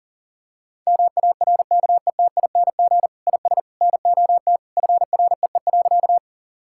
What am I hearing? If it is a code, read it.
Morse code, heard as a keyed tone, about 40 wpm, 700 Hz. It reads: MARKETING IS NOT FREE.
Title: Sentences from Top 1000 Words - Review Part 5 - 40wpm